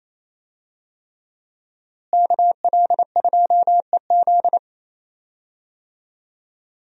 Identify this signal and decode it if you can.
Morse code, heard as a keyed tone, about 28 wpm, 700 Hz. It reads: KL2EZ
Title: USA Call Signs - 5 Characters - Repeated 3x - 28wpm